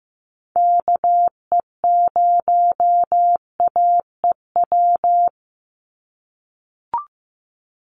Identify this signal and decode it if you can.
Morse code, heard as a keyed tone, about 15 wpm, 700 Hz. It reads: KE0AEW
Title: USA Call Signs 15wpm